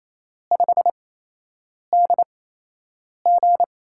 Morse code, heard as a keyed tone, about 27 wpm, spaced as for 8 wpm, 700 Hz. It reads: 5DG
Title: USA Call Signs - 8x Character Spacing - 28wpm